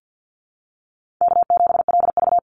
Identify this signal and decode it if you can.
Morse code, heard as a keyed tone, about 50 wpm, 700 Hz. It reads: K7LV
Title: Easy Call Signs Repeated 2x - 50wpm